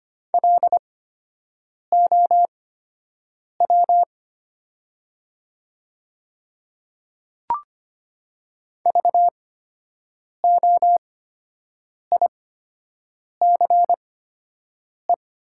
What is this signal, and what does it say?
Morse code, heard as a keyed tone, about 25 wpm, spaced as for 7 wpm, 700 Hz. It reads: LOW VOICE
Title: 8x Character Spacing - Top 500 Words - 25wpm